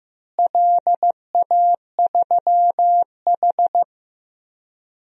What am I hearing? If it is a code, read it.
Morse code, heard as a keyed tone, about 15 wpm, 700 Hz. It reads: LA3H
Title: DXCC-100 Call Signs - 4 Characters - Repeated 3x - 15wpm